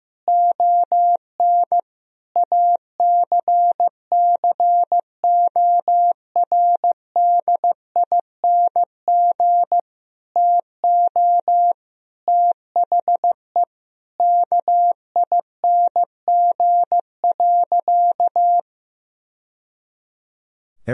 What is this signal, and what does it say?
Morse code, heard as a keyed tone, about 16 wpm, 700 Hz. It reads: ON ACCORDING TO THE KING.
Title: Sentences from Top 1000 Words - Review Part 4 - 15wpm